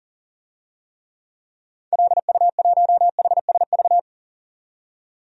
Morse code, heard as a keyed tone, about 40 wpm, 700 Hz. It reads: LU1HSV
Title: DXCC-100 Call Signs - 6 Characters - 40wpm